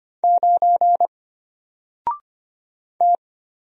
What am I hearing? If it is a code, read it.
Morse code, heard as a keyed tone, about 25 wpm, 700 Hz. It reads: 9 T
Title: Single Letter-Number Alt 25wpm